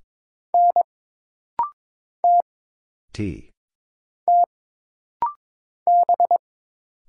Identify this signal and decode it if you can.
Morse code, heard as a keyed tone, about 22 wpm, 700 Hz. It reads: N T T B
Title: Single Letter-Number - Rapid-Fire 22wpm